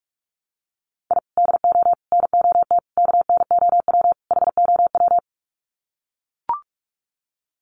Morse code, heard as a keyed tone, about 44 wpm, 700 Hz. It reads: I DO NOT KNOW HOW
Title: Sets of 5 Words 45wpm